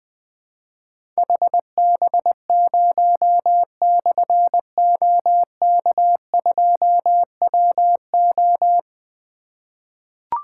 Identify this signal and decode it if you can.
Morse code, heard as a keyed tone, about 20 wpm, 700 Hz. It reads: HB0/OK2WO
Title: Difficult Call Signs 20wpm